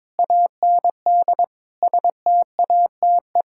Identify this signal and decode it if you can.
Morse code, heard as a keyed tone, about 22 wpm, 700 Hz. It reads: AND STATE
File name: practice / 22wpm / Build A Phrase - 5-Word Phrases - 22wpm